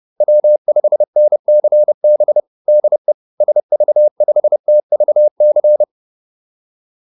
Morse code, heard as a keyed tone, about 30 wpm, 600 Hz. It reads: W5NCB DE SV5TVC